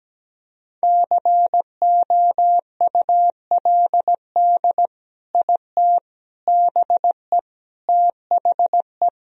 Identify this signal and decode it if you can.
Morse code, heard as a keyed tone, about 17 wpm, 700 Hz. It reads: COULD IT BE THE
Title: Sentences from Top 1000 Words - Review Part 1 - 17wpm